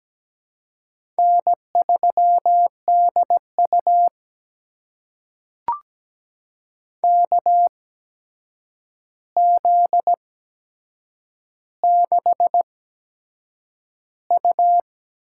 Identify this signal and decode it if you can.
Morse code, heard as a keyed tone, about 17 wpm, spaced as for 7 wpm, 700 Hz. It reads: N3DU KZ6U
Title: USA Call Signs - 4 Characters - 8x Character Spacing - 17wpm